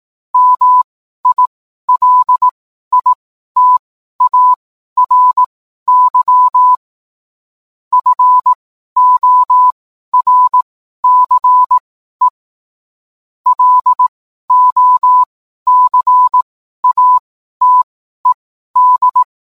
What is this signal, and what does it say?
Morse code, heard as a keyed tone, about 18 wpm, spaced as for 12 wpm, 1000 Hz. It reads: MILITARY FORCE LOCATED
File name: cwblog / cw mp3 directory / mem morse 12